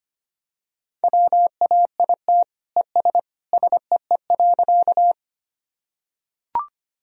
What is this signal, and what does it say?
Morse code, heard as a keyed tone, about 25 wpm, 700 Hz. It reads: WAIT ES SEE.
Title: Sentences from Top 500 Words and Abbreviations 25wpm